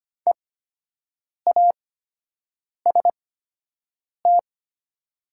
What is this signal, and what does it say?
Morse code, heard as a keyed tone, about 25 wpm, spaced as for 7 wpm, 690 Hz. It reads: EAST
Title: 8x Character Spacing - Top 500 Words - 25wpm